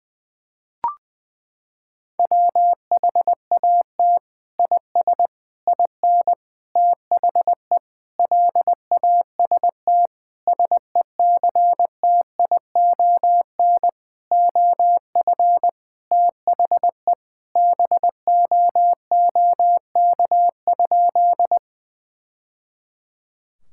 Morse code, heard as a keyed tone, about 21 wpm, 700 Hz. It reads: WHAT IS IN THE LAST SECTION OF THE BOOK?
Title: Sentences from Top 1000 Words - Review Part 5 - 20wpm